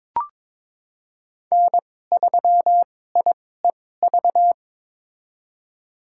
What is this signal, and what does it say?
Morse code, heard as a keyed tone, about 22 wpm, spaced as for 16 wpm, 700 Hz. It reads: N3IEV
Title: USA Call Signs - 5 Characters - 2x Character Spacing - 22wpm